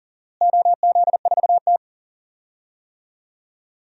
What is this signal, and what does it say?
Morse code, heard as a keyed tone, about 40 wpm, 700 Hz. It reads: OZ4T
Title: DXCC-100 Call Signs - 4 Characters - Repeated 2x - 40wpm